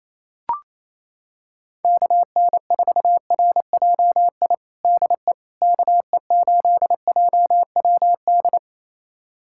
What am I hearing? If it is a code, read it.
Morse code, heard as a keyed tone, about 28 wpm, 700 Hz. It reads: KN4RJI DE KE8JWD
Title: USA Callsign DE Callsign - 6 Characters - 28wpm